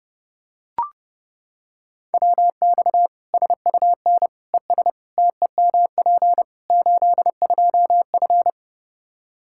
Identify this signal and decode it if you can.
Morse code, heard as a keyed tone, about 30 wpm, 700 Hz. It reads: WX SUN ES TEMP 82F
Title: K1USN SST QSO Practice 30wpm